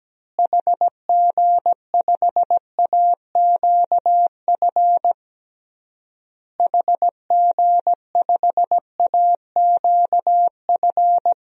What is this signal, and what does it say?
Morse code, heard as a keyed tone, about 17 wpm, 700 Hz. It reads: HG5AQF HG5AQF
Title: DXCC-100 Call Signs - 6 Characters - Repeated 3x - 17wpm